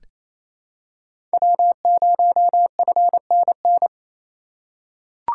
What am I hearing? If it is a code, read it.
Morse code, heard as a keyed tone, about 28 wpm, 700 Hz. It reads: W0FNN
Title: USA Call Signs - 5 Characters - 2x Character Spacing - 28wpm